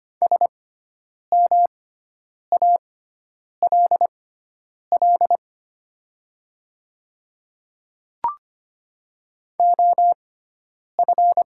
Morse code, heard as a keyed tone, about 25 wpm, spaced as for 9 wpm, 700 Hz. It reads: SMALL OF
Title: 6x Character Spacing - Top 200 Words - 25wpm